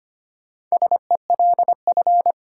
Morse code, heard as a keyed tone, about 25 wpm, 700 Hz. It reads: SELF
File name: practice / 25wpm / Difficult Sentences with Hints - 25wpm